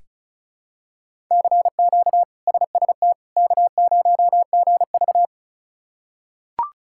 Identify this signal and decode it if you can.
Morse code, heard as a keyed tone, about 34 wpm, 700 Hz. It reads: CQ SST K0GV
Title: K1USN SST QSO Practice - 35wpm